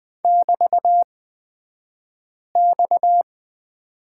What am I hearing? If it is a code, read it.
Morse code, heard as a keyed tone, about 20 wpm, spaced as for 5 wpm, 700 Hz. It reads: <BT>X
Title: 249 Lesson 40d - BK - Two Characters 20wpm